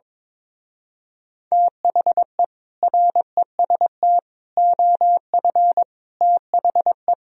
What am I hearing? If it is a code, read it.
Morse code, heard as a keyed tone, about 22 wpm, 700 Hz. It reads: THE REST OF THE
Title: Speed-Racing - Sets of 5 Words 22wpm